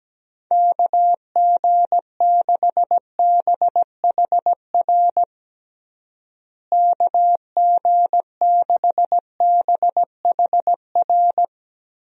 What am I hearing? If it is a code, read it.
Morse code, heard as a keyed tone, about 17 wpm, 700 Hz. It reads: KG6BHR KG6BHR
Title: USA Call Signs - 6 Characters - Repeated 3x - 17wpm